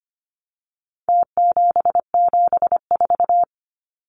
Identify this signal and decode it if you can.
Morse code, heard as a keyed tone, about 25 wpm, 700 Hz. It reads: T774
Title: Numbers - Contest Style - 4-Digit 25wpm